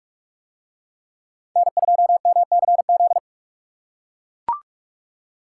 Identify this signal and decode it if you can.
Morse code, heard as a keyed tone, about 40 wpm, 700 Hz. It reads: N2MCZ